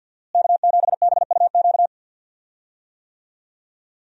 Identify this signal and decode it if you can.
Morse code, heard as a keyed tone, about 48 wpm, 700 Hz. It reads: K7BUQ
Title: USA Call Signs - 5 Characters - Repeated 2x - 50wpm